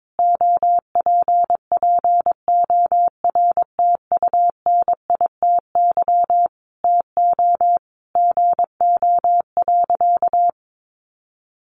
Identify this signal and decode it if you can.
Morse code, heard as a keyed tone, about 22 wpm, 700 Hz. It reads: OPPORTUNITY TO GO.
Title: Sentences from Top 1000 Words 22wpm